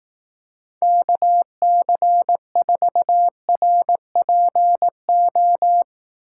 Morse code, heard as a keyed tone, about 18 wpm, 700 Hz. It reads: KC4RPO
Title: Speed Racing - USA 2x3 Call Signs 15wpm